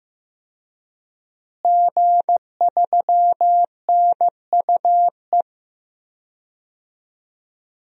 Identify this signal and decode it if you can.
Morse code, heard as a keyed tone, about 15 wpm, 700 Hz. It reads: G3NUE
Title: DXCC-100 Call Signs - 5 Characters - Repeated 3x - 15wpm